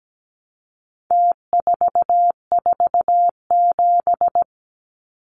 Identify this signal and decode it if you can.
Morse code, heard as a keyed tone, about 17 wpm, 700 Hz. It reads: T447
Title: Numbers - Contest Style - 4-Digit 17wpm